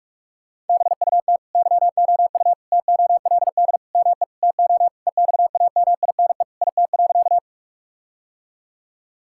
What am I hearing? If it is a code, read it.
Morse code, heard as a keyed tone, about 47 wpm, 700 Hz. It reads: BUT YOU TOLD ME TO EXAMINE IT.
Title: Sentences from Top 1600 Words 45wpm